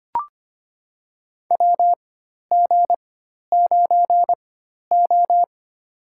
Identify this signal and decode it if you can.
Morse code, heard as a keyed tone, about 25 wpm, spaced as for 12 wpm, 700 Hz. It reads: WG9O
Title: USA Call Signs - 4 Characters - 4x Character Spacing - 25wpm